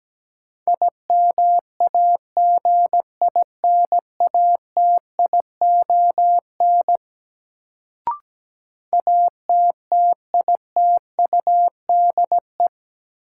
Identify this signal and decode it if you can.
Morse code, heard as a keyed tone, about 17 wpm, 700 Hz. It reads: IMAGINATION ATTITUDE
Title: Sentences from Top 1300 Words - New Word Drill Alt - 17wpm